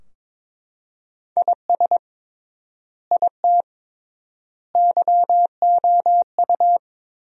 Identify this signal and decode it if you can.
Morse code, heard as a keyed tone, about 22 wpm, 700 Hz. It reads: IS IT YOU
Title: 3x Word-Spacing - Sentences from Top 200 Words - 22wpm